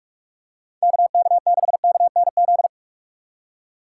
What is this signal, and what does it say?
Morse code, heard as a keyed tone, about 44 wpm, 700 Hz. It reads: KK6KNZ
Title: USA Call Signs - 6 Characters - 4x Character Spacing - 45wpm